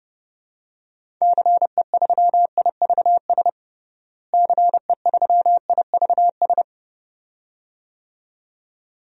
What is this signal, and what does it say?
Morse code, heard as a keyed tone, about 31 wpm, 700 Hz. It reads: CE3IVS CE3IVS
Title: DXCC-100 Call Signs - 6 Characters - Repeated 2x - 30wpm